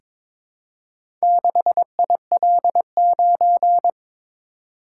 Morse code, heard as a keyed tone, about 22 wpm, 690 Hz. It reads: 6IL9
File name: practice / 22wpm / Four Letter-Number Combinations - 22wpm